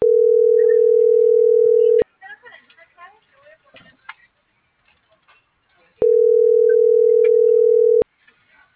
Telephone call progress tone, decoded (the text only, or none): ringback tone